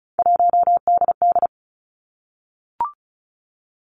Morse code, heard as a keyed tone, about 36 wpm, 700 Hz. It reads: 1DD
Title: Three Challenging Letter-Number Combinations 35wpm